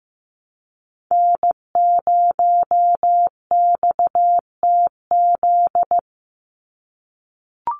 Morse code, heard as a keyed tone, about 15 wpm, 700 Hz. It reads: N0XTZ
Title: USA Call Signs 15wpm